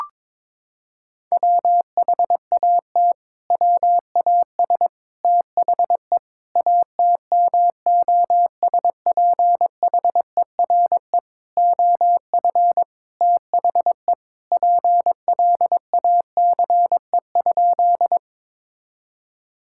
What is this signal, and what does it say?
Morse code, heard as a keyed tone, about 22 wpm, 700 Hz. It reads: WHAT WAS THE ATMOSPHERE OF THE PLACE?
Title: Sentences from Top 1600 Words 22wpm